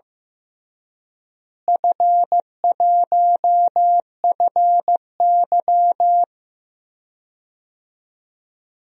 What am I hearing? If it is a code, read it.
Morse code, heard as a keyed tone, about 16 wpm, 700 Hz. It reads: F1FY
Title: DXCC-100 Call Signs - 4 Characters - Repeated 3x - 15wpm